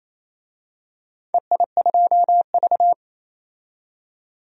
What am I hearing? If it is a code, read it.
Morse code, heard as a keyed tone, about 28 wpm, 700 Hz. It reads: EI2V
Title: DXCC-100 Call Signs - 4 Characters - 28wpm